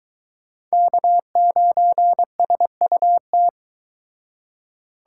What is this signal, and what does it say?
Morse code, heard as a keyed tone, about 23 wpm, 700 Hz. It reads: K9SUT